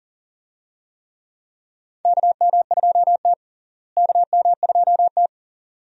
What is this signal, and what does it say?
Morse code, heard as a keyed tone, about 40 wpm, 700 Hz. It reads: KM2T KM2T
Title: USA Call Signs - 4 Characters - Repeated 2x - 40wpm